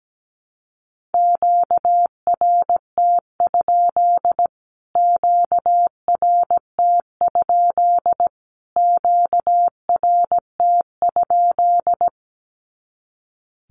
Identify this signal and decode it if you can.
Morse code, heard as a keyed tone, about 17 wpm, 700 Hz. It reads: QRT? QRT? QRT?
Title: Q-Codes Repeated 3x 17wpm